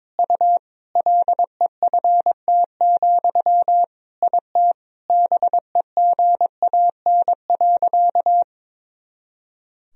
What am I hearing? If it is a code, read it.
Morse code, heard as a keyed tone, about 21 wpm, 700 Hz. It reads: U LEFT, IT BEGAN.
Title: Sentences from Top 400 Words and Abbreviations 22wpm